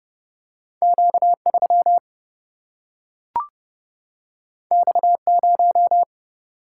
Morse code, heard as a keyed tone, about 31 wpm, 700 Hz. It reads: Q 3 X 0
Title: Single-Letter Single-Number 30wpm